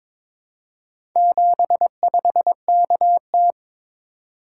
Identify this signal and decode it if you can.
Morse code, heard as a keyed tone, about 22 wpm, 700 Hz. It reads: 75KT
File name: practice / 22wpm / Four Letter-Number Combinations - 22wpm